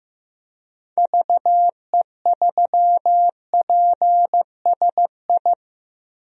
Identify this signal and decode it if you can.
Morse code, heard as a keyed tone, about 15 wpm, 700 Hz. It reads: VE3PSI
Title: DXCC-100 Call Signs - 6 Characters - Repeated 2x - 15wpm